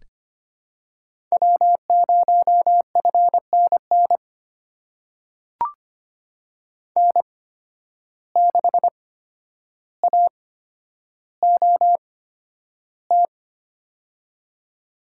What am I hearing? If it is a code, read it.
Morse code, heard as a keyed tone, about 25 wpm, spaced as for 11 wpm, 700 Hz. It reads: W0FNN N6AOT
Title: USA Call Signs - 5 Characters - 8x Character Spacing - 25wpm